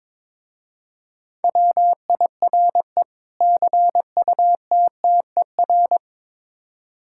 Morse code, heard as a keyed tone, about 21 wpm, 700 Hz. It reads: WIRE CUTTER
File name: Difficult Words with Hints - 22wpm